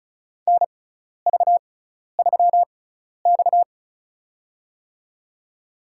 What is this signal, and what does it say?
Morse code, heard as a keyed tone, about 36 wpm, spaced as for 12 wpm, 700 Hz. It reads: NV3X
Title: USA Call Signs - 4 Characters - 6x Character Spacing - 35wpm